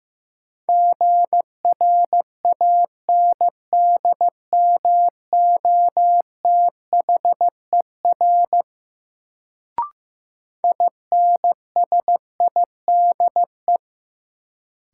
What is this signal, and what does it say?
Morse code, heard as a keyed tone, about 16 wpm, 700 Hz. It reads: GRANDMOTHER INSIDE
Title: Sentences from Top 1200 Words - New Word Drill Alt - 15wpm